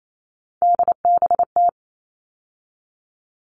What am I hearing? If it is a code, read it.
Morse code, heard as a keyed tone, about 28 wpm, 700 Hz. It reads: DBT
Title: All Cut Numbers - 3-Digit 28wpm